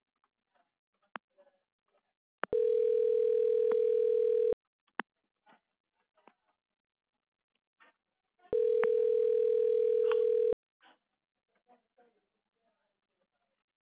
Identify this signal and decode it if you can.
Ringback tone (North America), 440 + 480 Hz.